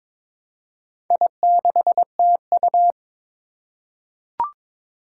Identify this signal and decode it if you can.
Morse code, heard as a keyed tone, about 22 wpm, 700 Hz. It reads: I6TU
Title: Four Challenging Letter-Number Combinations 22wpm